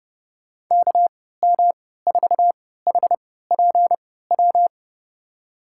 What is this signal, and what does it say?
Morse code, heard as a keyed tone, about 30 wpm, spaced as for 17 wpm, 700 Hz. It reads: KM4HPW
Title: USA Call Signs - 6 Characters - 3x Character Spacing - 30wpm